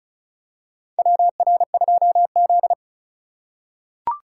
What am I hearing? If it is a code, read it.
Morse code, heard as a keyed tone, about 36 wpm, 700 Hz. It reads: WR2Z
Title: USA Call Signs - 3x Character Spacing - 35wpm